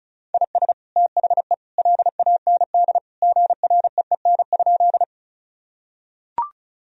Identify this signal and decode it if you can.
Morse code, heard as a keyed tone, about 36 wpm, 700 Hz. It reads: IS THE LAND GREEN?
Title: Sentences from Top 1000 Words - Review Part 2 - 35wpm